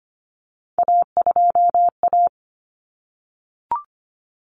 Morse code, heard as a keyed tone, about 25 wpm, 700 Hz. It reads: A2A
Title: Common Cut Numbers - 3-Digit 25wpm